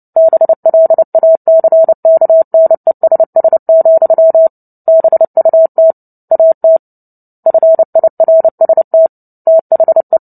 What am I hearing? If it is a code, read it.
Morse code, heard as a keyed tone, about 30 wpm, 650 Hz. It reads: BLACKNESS, BUT AT FIRST THE
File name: Book Two / 03-Chapter 3 - 29 WPM